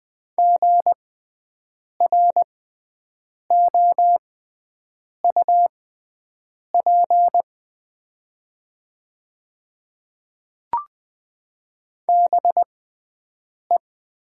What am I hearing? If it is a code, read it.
Morse code, heard as a keyed tone, about 20 wpm, spaced as for 7 wpm, 700 Hz. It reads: GROUP BE
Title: 6x Character Spacing - Top 300 Words - 20wpm